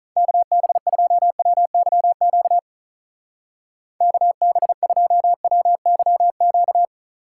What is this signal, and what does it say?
Morse code, heard as a keyed tone, about 44 wpm, 700 Hz. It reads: KB2WYQ KB2WYQ